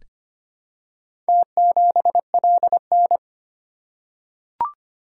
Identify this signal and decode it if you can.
Morse code, heard as a keyed tone, about 25 wpm, 700 Hz. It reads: T7LN
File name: practice / 25wpm / Four Letter-Number Combinations - 25wpm